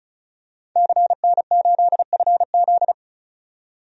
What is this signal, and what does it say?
Morse code, heard as a keyed tone, about 36 wpm, 700 Hz. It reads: CN8FZ
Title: DXCC-100 Call Signs - 5 Characters - 35wpm